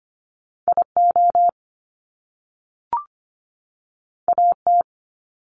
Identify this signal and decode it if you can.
Morse code, heard as a keyed tone, about 25 wpm, 700 Hz. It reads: IO AT
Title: Top Two Letter Combinations in Words 25wpm